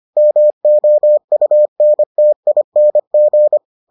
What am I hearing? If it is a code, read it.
Morse code, heard as a keyed tone, about 25 wpm, 600 Hz. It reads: MOUNTING